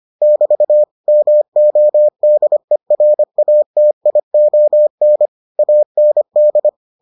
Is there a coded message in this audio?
Yes: Morse code from 0.2 to 6.7 s, heard as a keyed tone, about 25 wpm, 600 Hz. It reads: <BT> MODERATION AND